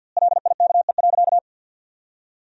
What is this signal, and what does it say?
Morse code, heard as a keyed tone, about 52 wpm, 700 Hz. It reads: LIKE.
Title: 2x Word-Spacing - Sentences from Top 500 Words - 50wpm